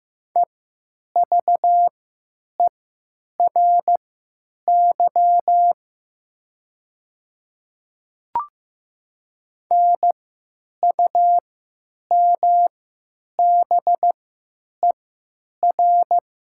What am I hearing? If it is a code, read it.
Morse code, heard as a keyed tone, about 15 wpm, spaced as for 8 wpm, 700 Hz. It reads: EVERY NUMBER
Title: 3x Character Spacing - Top 300 Words - 15wpm